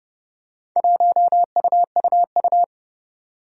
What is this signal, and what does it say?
Morse code, heard as a keyed tone, about 32 wpm, 700 Hz. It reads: 1UUU
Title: Four Challenging Letter-Number Combinations 30wpm